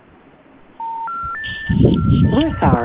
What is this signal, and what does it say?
Special information tone, with speech.